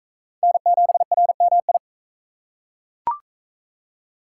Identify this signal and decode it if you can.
Morse code, heard as a keyed tone, about 40 wpm, 700 Hz. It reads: N7RMI